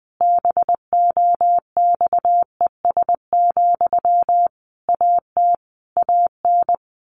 Morse code, heard as a keyed tone, about 20 wpm, 700 Hz. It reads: BOXES, AT AN
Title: A Sound Of Thunder 20wpm